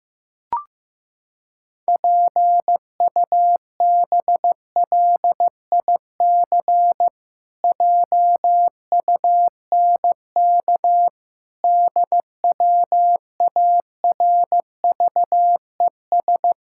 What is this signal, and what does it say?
Morse code, heard as a keyed tone, about 15 wpm, 700 Hz. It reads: PUBLIC JUNK DWARVES
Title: Pangram Challenge v2 - 15wpm